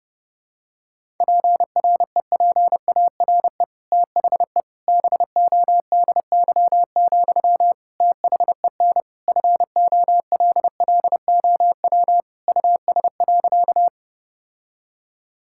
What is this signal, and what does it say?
Morse code, heard as a keyed tone, about 30 wpm, 700 Hz. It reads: PREPARE THE BODY, THEN FOLLOW US.
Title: Sentences from Top 1000 Words - Review Part 6 - 30wpm